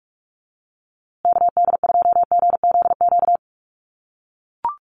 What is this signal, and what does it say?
Morse code, heard as a keyed tone, about 44 wpm, 700 Hz. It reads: KD2GZQ
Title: USA Call Signs Repeated 3x 45wpm